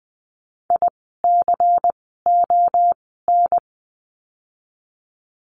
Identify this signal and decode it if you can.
Morse code, heard as a keyed tone, about 20 wpm, spaced as for 14 wpm, 700 Hz. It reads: ICON
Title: 094 Lesson 16f - Letter c - Words 20wpm